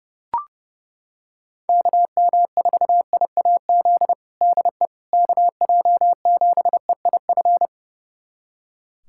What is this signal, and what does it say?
Morse code, heard as a keyed tone, about 32 wpm, 700 Hz. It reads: KM4IAZ DE KJ7EIF
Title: USA Callsign DE Callsign 30wpm